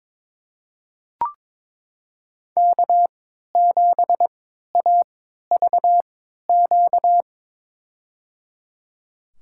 Morse code, heard as a keyed tone, about 22 wpm, spaced as for 12 wpm, 700 Hz. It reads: K7AVQ
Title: USA Call Signs - 5 Characters - 3x Character Spacing - 22wpm